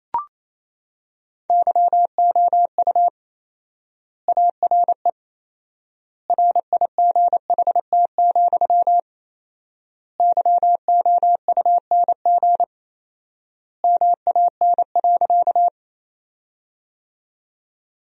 Morse code, heard as a keyed tone, about 28 wpm, 700 Hz. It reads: YOU ARE RIGHT, YOUNG MAN.
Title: 4x Word-Spacing - Sentences from Top 400 Words - 28wpm